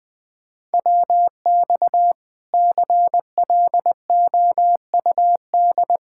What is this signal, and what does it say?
Morse code, heard as a keyed tone, about 20 wpm, 700 Hz. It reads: WX CLOUD
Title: Speed-Racing Common QSO Phrases - 17wpm